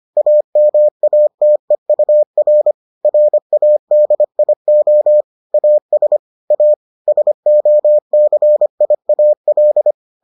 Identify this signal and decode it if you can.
Morse code, heard as a keyed tone, about 25 wpm, 600 Hz. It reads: AMATEUR RADIO AS A SOCIAL